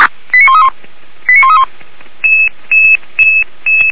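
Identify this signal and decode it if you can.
Five-tone selective call (ZVEI), heard as quick runs of five tones, tones of about 70 ms. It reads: 89131 89131
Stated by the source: ZVEI five-tone call received by SDR.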